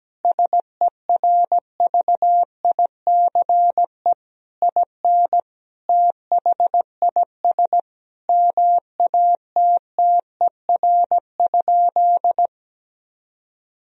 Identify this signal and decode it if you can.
Morse code, heard as a keyed tone, about 17 wpm, 700 Hz. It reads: SERVICE IN THIS MATTER?
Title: Sentences from Top 1000 Words - Review Part 4 - 17wpm